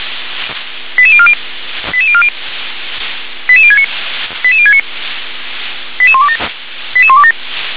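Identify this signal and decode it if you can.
Five-tone selective call (ZVEI), heard as quick runs of five tones, tones of about 70 ms. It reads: 88040 88040 88069 88069 88127 88127
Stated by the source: ZVEI five-tone call received with an SDR.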